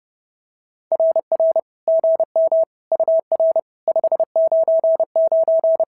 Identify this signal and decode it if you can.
Morse code, heard as a keyed tone, about 30 wpm, 650 Hz. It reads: RR GM UR 599
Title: SOTA QSO Practice 30wpm